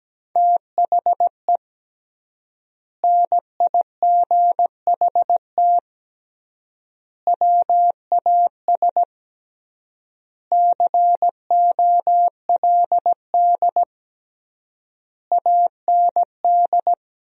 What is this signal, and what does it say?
Morse code, heard as a keyed tone, about 17 wpm, 700 Hz. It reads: THE NIGHT WAS COLD AND
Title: 3x Word-Spacing - Sentences from Top 400 Words - 17wpm